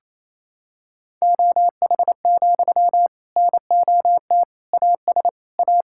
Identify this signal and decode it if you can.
Morse code, heard as a keyed tone, about 28 wpm, 700 Hz. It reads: OH, NOT AS A
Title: Sentences from Top 1000 Words - Review Part 5 - 28wpm